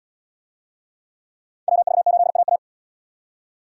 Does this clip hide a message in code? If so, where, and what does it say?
Yes: Morse code from 1.7 to 2.6 s, heard as a keyed tone, about 68 wpm, 700 Hz. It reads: WV8AA